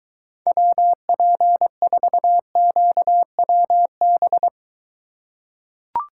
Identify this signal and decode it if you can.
Morse code, heard as a keyed tone, about 23 wpm, 700 Hz. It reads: WP4QWB